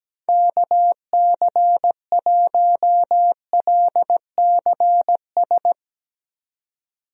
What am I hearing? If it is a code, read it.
Morse code, heard as a keyed tone, about 17 wpm, 700 Hz. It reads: KC1LCS
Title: USA Call Signs - 6 Characters - 17wpm